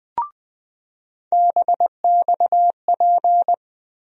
Morse code, heard as a keyed tone, about 20 wpm, 700 Hz. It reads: BXP
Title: USA Call Sign Postfixes - 3 Letters 20wpm